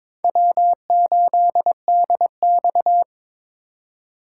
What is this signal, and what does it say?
Morse code, heard as a keyed tone, about 26 wpm, 700 Hz. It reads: W8DX
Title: Speed Racing - USA 1x2 Call Signs 22wpm